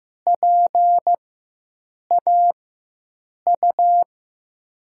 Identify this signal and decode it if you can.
Morse code, heard as a keyed tone, about 15 wpm, spaced as for 7 wpm, 700 Hz. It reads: PAU
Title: USA Call Signs - 6 Characters - 4x Character Spacing - 15wpm